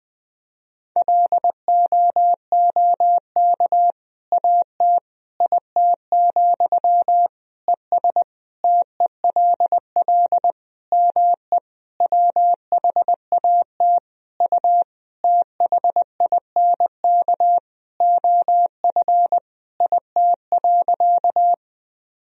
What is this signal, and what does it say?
Morse code, heard as a keyed tone, about 21 wpm, 700 Hz. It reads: LOOK AT IT, ES TELL ME WHAT U THINK OF IT.
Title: Sentences from Top 200 Words and Abbreviations 20wpm